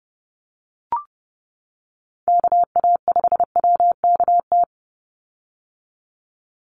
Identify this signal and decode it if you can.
Morse code, heard as a keyed tone, about 28 wpm, 700 Hz. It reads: KA5WKT
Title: USA Call Signs 30wpm